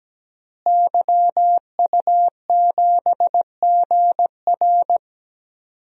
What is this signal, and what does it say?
Morse code, heard as a keyed tone, about 17 wpm, 700 Hz. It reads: YU7GR
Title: DXCC-100 Call Signs - 5 Characters - Repeated 2x - 17wpm